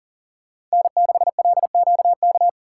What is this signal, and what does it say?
Morse code, heard as a keyed tone, about 37 wpm, 700 Hz. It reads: N6LQK
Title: USA Call Signs - 5 Characters - 3x Character Spacing - 40wpm